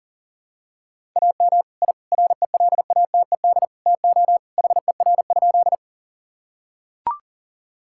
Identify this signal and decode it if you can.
Morse code, heard as a keyed tone, about 40 wpm, 700 Hz. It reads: AM I RELATED TO HER?
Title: Sentences from Top 1000 Words - Review Part 5 - 40wpm